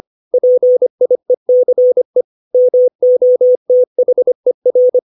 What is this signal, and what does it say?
Morse code, heard as a keyed tone, about 25 wpm, 500 Hz. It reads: PIECE MOTHER
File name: 632 words / 632 Words 25WPM